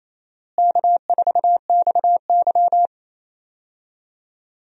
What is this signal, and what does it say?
Morse code, heard as a keyed tone, about 28 wpm, 700 Hz. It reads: K4XY
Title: USA Call Signs - 4 Characters - Repeated 3x - 28wpm